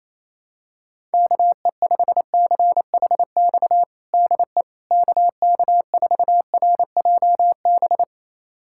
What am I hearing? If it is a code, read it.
Morse code, heard as a keyed tone, about 28 wpm, 700 Hz. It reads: KE5CHX DE KK4RJB
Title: USA Callsign DE Callsign - 6 Characters - 28wpm